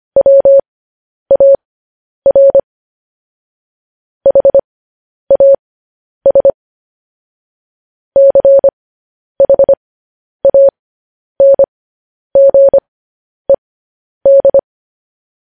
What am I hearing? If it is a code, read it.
Morse code, heard as a keyed tone, about 25 wpm, spaced as for 10 wpm, 550 Hz. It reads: WAR HAS CHANGED
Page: news headlines in morse code at 10 wpm